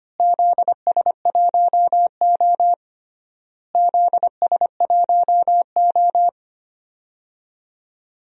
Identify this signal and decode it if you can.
Morse code, heard as a keyed tone, about 25 wpm, 700 Hz. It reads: ZS1O ZS1O
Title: DXCC-100 Call Signs - 4 Characters - Repeated 3x - 25wpm